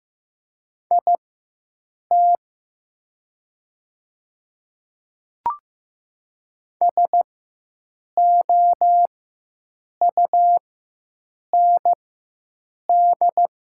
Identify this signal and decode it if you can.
Morse code, heard as a keyed tone, about 15 wpm, spaced as for 7 wpm, 700 Hz. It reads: IT SOUND
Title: 4x Character Spacing - Top 100 Words - 15wpm